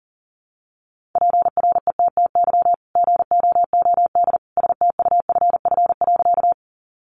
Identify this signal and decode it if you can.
Morse code, heard as a keyed tone, about 40 wpm, 700 Hz. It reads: PRETTY GOOD STUFF.
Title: Sentences from Top 1400 Words 40wpm